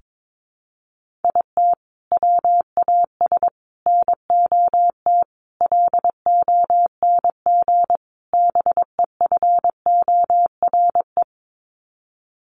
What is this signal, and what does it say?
Morse code, heard as a keyed tone, about 22 wpm, 700 Hz. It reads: IT WAS NOT LONG BEFORE
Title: Sets of 5 Words Encore2 22wpm